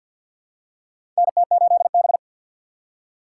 Morse code, heard as a keyed tone, about 53 wpm, 700 Hz. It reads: NT8B
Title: USA Call Signs - 6x Character Spacing - 50wpm